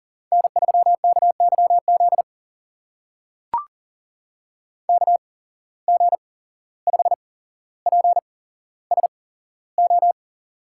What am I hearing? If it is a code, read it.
Morse code, heard as a keyed tone, about 39 wpm, spaced as for 16 wpm, 700 Hz. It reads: N3KYZ KG5PSO
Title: USA Call Signs - 8x Character Spacing - 40wpm